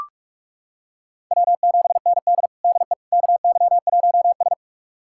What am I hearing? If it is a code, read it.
Morse code, heard as a keyed tone, about 41 wpm, 700 Hz. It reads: W7ND DE KY1S
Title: USA Callsign DE Callsign - 4 Characters - 45wpm